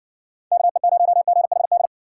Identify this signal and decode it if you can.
Morse code, heard as a keyed tone, about 54 wpm, 700 Hz. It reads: KE0GHD